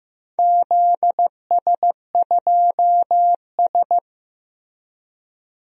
Morse code, heard as a keyed tone, about 15 wpm, 700 Hz. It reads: ZS2S
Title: DXCC-100 Call Signs - 4 Characters - Repeated 2x - 15wpm